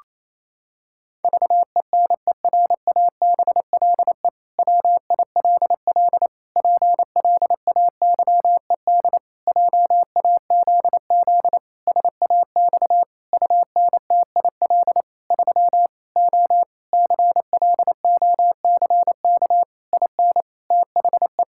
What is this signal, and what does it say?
Morse code, heard as a keyed tone, about 28 wpm, 700 Hz. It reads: VENERABLE WILL PLAYED JAZZ SAX UNTIL 3 O CLOCK IN THE
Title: Pangram Challenge v2 - 28wpm